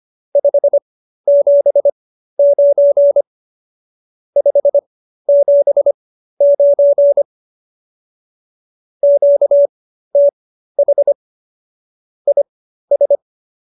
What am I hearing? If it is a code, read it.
Morse code, heard as a keyed tone, about 25 wpm, spaced as for 13 wpm, 570 Hz. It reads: 579 579 QTH IS